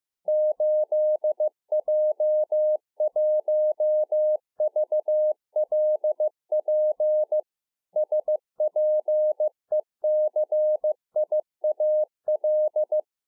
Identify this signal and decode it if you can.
Morse code, heard as a keyed tone, about 15 wpm, 610 Hz. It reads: 8J1VLP SPECIAL